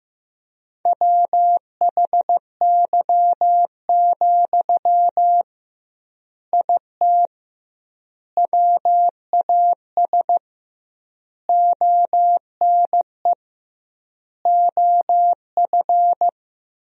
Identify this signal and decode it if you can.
Morse code, heard as a keyed tone, about 16 wpm, 700 Hz. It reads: WHY, IT WAS ONE OF
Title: 2x Word-Spacing - Sentences from Top 300 Words - 15wpm